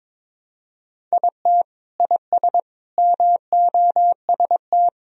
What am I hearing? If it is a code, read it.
Morse code, heard as a keyed tone, about 22 wpm, 700 Hz. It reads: IT IS MOST
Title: Sentences from Top 1000 Words - Review Part 1 - 22wpm